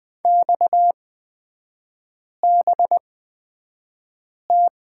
Morse code, heard as a keyed tone, about 20 wpm, spaced as for 5 wpm, 700 Hz. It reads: XBT
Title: 229 Lesson 37e - Letter x - Three Characters 20wpm